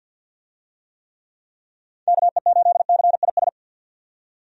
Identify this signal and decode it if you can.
Morse code, heard as a keyed tone, about 51 wpm, 700 Hz. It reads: KE8CIS
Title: USA Call Signs - 6 Characters - 50wpm